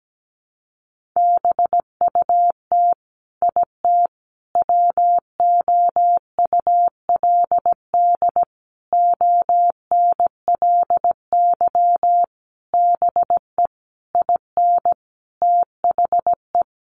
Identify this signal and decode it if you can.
Morse code, heard as a keyed tone, about 17 wpm, 700 Hz. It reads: BUT IT WOULD ONLY BE IN THE
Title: Sentences from Top 1500 Words 17wpm